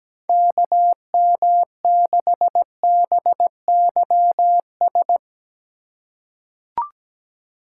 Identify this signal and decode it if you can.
Morse code, heard as a keyed tone, about 17 wpm, 700 Hz. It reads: KM6BYS
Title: USA Call Signs - 3x Character Spacing - 17wpm